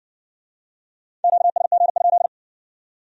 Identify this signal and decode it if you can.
Morse code, heard as a keyed tone, about 54 wpm, 700 Hz. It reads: QSG?